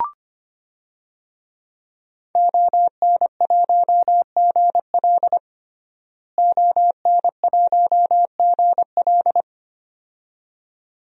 Morse code, heard as a keyed tone, about 25 wpm, 700 Hz. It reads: ON1GL ON1GL
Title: DXCC-100 Call Signs - 5 Characters - Repeated 2x - 25wpm